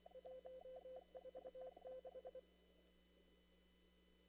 Morse code, heard as a keyed tone, about 27 wpm, 520 Hz. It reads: 146
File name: CW / kt5j wpx cw-op-k5pi